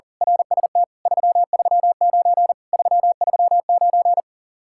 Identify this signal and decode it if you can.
Morse code, heard as a keyed tone, about 39 wpm, 700 Hz. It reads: RST 339 339
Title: Speed-Racing Common QSO Phrases Encore - 40wpm